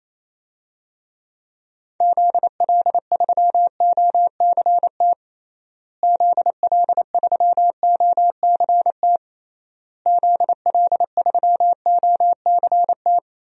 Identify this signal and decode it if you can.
Morse code, heard as a keyed tone, about 28 wpm, 700 Hz. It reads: ZL3OCT ZL3OCT ZL3OCT
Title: DXCC-100 Call Signs - 6 Characters - Repeated 3x - 28wpm